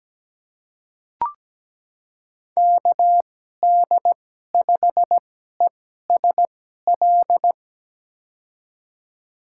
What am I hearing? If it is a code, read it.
Morse code, heard as a keyed tone, about 17 wpm, spaced as for 12 wpm, 700 Hz. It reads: KD5ESL
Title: USA Call Signs - 6 Characters - 2x Character Spacing - 17wpm